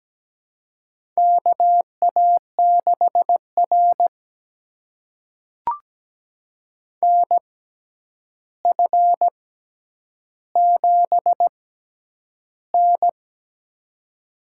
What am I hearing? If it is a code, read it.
Morse code, heard as a keyed tone, about 17 wpm, spaced as for 9 wpm, 700 Hz. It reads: KA6R NF7N
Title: USA Call Signs - 6x Character Spacing - 17wpm